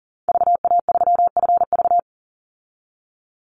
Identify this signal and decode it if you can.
Morse code, heard as a keyed tone, about 40 wpm, 700 Hz. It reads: VA3FV
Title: Call Signs Repeated 2x - 40wpm